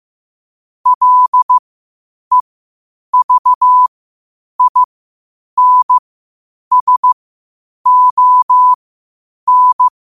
Morse code, heard as a keyed tone, about 15 wpm, spaced as for 8 wpm, 1000 Hz. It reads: LEVINSON
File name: qotdfiles / 8wpm